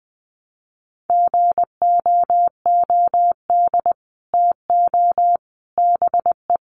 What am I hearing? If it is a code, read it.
Morse code, heard as a keyed tone, about 19 wpm, 700 Hz. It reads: GOOD TO BE
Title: Sets of 5 Words Encore2 20wpm